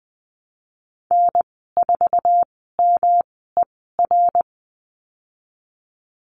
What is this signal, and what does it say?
Morse code, heard as a keyed tone, about 20 wpm, spaced as for 15 wpm, 700 Hz. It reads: N4MER
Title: 122 Lesson 20g - Number 3 - Call Signs 20wpm